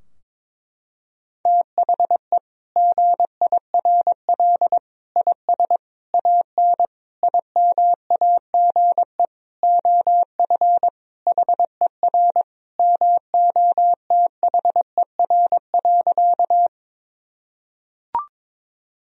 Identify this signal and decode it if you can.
Morse code, heard as a keyed tone, about 22 wpm, 700 Hz. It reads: THE GIRL IS AN IMAGE OF HER MOTHER.
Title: Sentences from Top 1000 Words - Review Part 4 - 22wpm